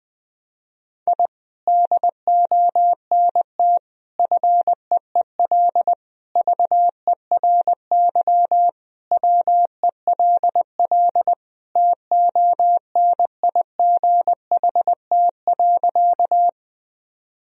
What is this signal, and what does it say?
Morse code, heard as a keyed tone, about 20 wpm, 700 Hz. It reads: I DONT FEEL VERY WELL TONIGHT.
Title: Sentences from Top 1000 Words - Review Part 3 - 20wpm